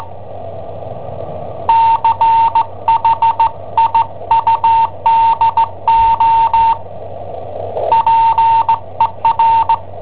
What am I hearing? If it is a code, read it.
Morse code, heard as a keyed tone, about 17 wpm, 910 Hz. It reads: CHIUDO PER
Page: Cassetta esercizi preparata da I7OHP per IW5ELP